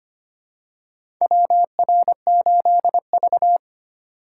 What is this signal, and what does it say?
Morse code, heard as a keyed tone, about 25 wpm, 700 Hz. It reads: WR8V
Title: USA Call Signs - 4 Characters - 2x Character Spacing - 25wpm